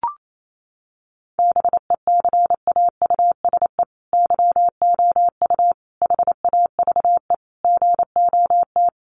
Morse code, heard as a keyed tone, about 28 wpm, 700 Hz. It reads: BECAUSE YOU HAVE GOT
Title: Sentences from Top 1500 Words 28wpm